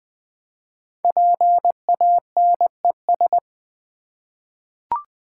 Morse code, heard as a keyed tone, about 20 wpm, 700 Hz. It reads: PANES
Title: Code Groups - 5 Letters - 20wpm